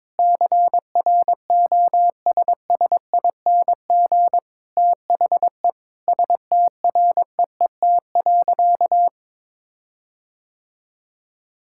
Morse code, heard as a keyed tone, about 22 wpm, 700 Hz. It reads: CROSSING THE STREET.
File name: Difficult Sentences with Hints - 22wpm